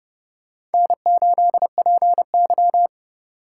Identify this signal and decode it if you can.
Morse code, heard as a keyed tone, about 30 wpm, 700 Hz. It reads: N8PY
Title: USA Call Signs - 4 Characters - 8x Character Spacing - 30wpm